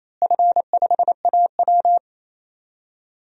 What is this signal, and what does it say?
Morse code, heard as a keyed tone, about 28 wpm, 700 Hz. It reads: F5AW
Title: DXCC-100 Call Signs - 4 Characters - Repeated 2x - 28wpm